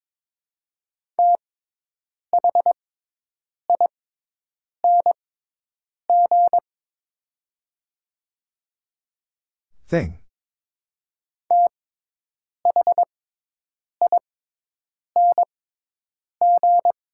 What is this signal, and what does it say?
Morse code, heard as a keyed tone, about 22 wpm, spaced as for 8 wpm, 700 Hz. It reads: THING THING
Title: 6x Character Spacing - Top 500 Words - 22wpm